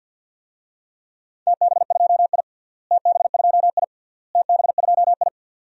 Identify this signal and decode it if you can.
Morse code, heard as a keyed tone, about 46 wpm, 700 Hz. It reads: TB2I TB2I TB2I
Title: DXCC-100 Call Signs - 4 Characters - Repeated 3x - 50wpm